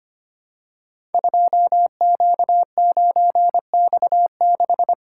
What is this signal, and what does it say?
Morse code, heard as a keyed tone, about 25 wpm, 700 Hz. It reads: 2Q9X6
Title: Code Groups - 5 Letter:Numbers - 25wpm